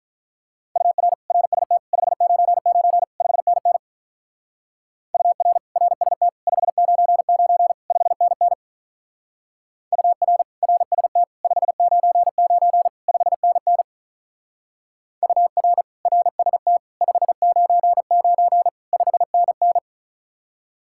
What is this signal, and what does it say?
Morse code, heard as a keyed tone, about 45 wpm, 700 Hz. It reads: UR RST 599 5NN UR RST 599 5NN UR RST 599 5NN UR RST 599 5NN